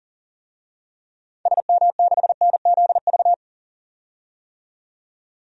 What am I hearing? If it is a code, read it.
Morse code, heard as a keyed tone, about 40 wpm, 700 Hz. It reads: SM6NZV
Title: DXCC-100 Call Signs - 6 Characters - 40wpm